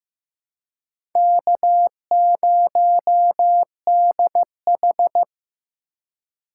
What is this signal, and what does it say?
Morse code, heard as a keyed tone, about 15 wpm, 700 Hz. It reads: K0DH
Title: USA Call Signs - 2x Character Spacing - 15wpm